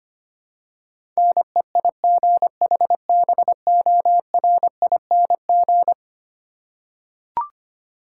Morse code, heard as a keyed tone, about 25 wpm, 700 Hz. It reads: NEIGHBORING
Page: morse-code-ninja-practice-25wpm